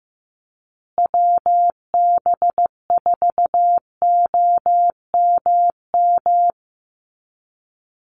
Morse code, heard as a keyed tone, about 15 wpm, 700 Hz. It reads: WB4OMM
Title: Call Signs Repeated 3x - 15wpm